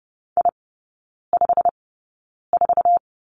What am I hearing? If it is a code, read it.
Morse code, heard as a keyed tone, about 30 wpm, spaced as for 9 wpm, 700 Hz. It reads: I54
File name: course / 30:10wpm / 087 Lesson 15e - Number 5 - Three Characters 30wpm